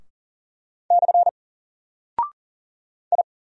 Morse code, heard as a keyed tone, about 40 wpm, 700 Hz. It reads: / I
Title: Single Character (Letter:Number:Punctuation:Prosign) Alt 40wpm